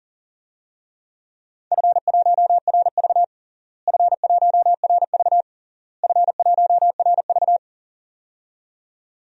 Morse code, heard as a keyed tone, about 40 wpm, 700 Hz. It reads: F1RV F1RV F1RV
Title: DXCC-100 Call Signs - 4 Characters - Repeated 3x - 40wpm